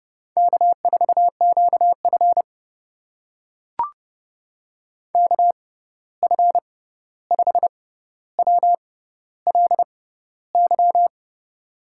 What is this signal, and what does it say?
Morse code, heard as a keyed tone, about 31 wpm, spaced as for 14 wpm, 700 Hz. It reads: K4QF KF5WLY
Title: USA Call Signs - 6x Character Spacing - 30wpm